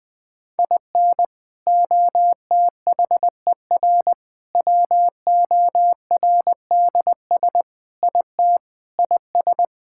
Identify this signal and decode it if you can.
Morse code, heard as a keyed tone, about 20 wpm, 700 Hz. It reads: IN OTHER WORDS IT IS
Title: Sentences with 3-Word Phrases - 20wpm